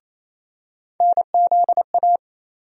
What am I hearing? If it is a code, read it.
Morse code, heard as a keyed tone, about 28 wpm, 700 Hz. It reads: NZA
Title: USA Call Sign Postfixes - 3 Letters 28wpm